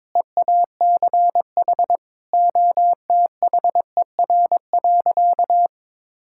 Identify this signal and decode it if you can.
Morse code, heard as a keyed tone, about 22 wpm, 700 Hz. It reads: EACH OTHER.
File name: practice / 22wpm / Sentences with 2-Word Phrases - 22wpm